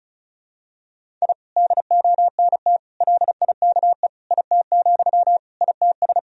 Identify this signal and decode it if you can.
Morse code, heard as a keyed tone, about 36 wpm, 700 Hz. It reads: I DONT LIKE IT, ITS
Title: Sentences from Top 1000 Words - Review Part 1 - 35wpm